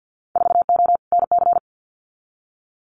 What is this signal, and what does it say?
Morse code, heard as a keyed tone, about 50 wpm, 700 Hz. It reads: 4O NC
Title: Winter Field Day Exchanges 50wpm